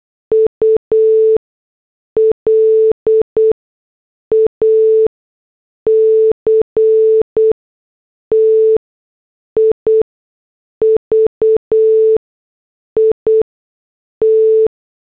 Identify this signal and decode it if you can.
Morse code, heard as a keyed tone, about 8 wpm, spaced as for 6 wpm, 440 Hz. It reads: ULACTIVIT